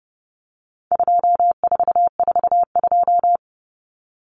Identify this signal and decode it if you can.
Morse code, heard as a keyed tone, about 30 wpm, 700 Hz. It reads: 2442
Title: Numbers - 4-Digit 30wpm